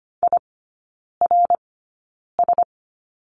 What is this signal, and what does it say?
Morse code, heard as a keyed tone, about 25 wpm, spaced as for 9 wpm, 690 Hz. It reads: IRS
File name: Lesson 15e - Number 5 - Three Characters 25wpm